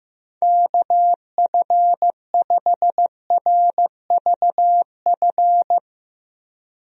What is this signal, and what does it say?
Morse code, heard as a keyed tone, about 15 wpm, 700 Hz. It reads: KF5RVF
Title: USA Call Signs - 6 Characters - 15wpm